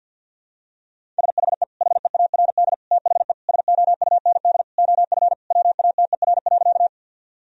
Morse code, heard as a keyed tone, about 49 wpm, 700 Hz. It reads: SHE HEARD THE SOUND OF WATER.
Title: Sentences from Top 1000 Words - Review Part 1 - 50wpm